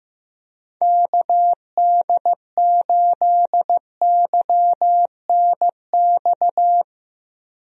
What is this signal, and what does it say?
Morse code, heard as a keyed tone, about 15 wpm, 700 Hz. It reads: KD8YNX
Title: USA Call Signs - 6 Characters - Repeated 2x - 15wpm